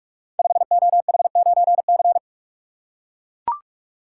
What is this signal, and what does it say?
Morse code, heard as a keyed tone, about 44 wpm, 700 Hz. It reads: 5OH9C
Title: Code Groups - 5 Letter:Numbers - 45wpm